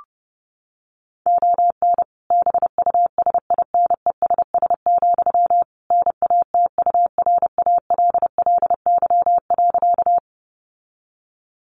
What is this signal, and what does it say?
Morse code, heard as a keyed tone, about 30 wpm, 700 Hz. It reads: ON BUSINESS, NATURALLY.
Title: Sentences from Top 1300 Words 30wpm